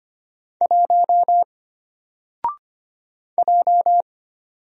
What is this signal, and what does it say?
Morse code, heard as a keyed tone, about 25 wpm, 700 Hz. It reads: 1 J
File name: Single-Letter-Number-25wpm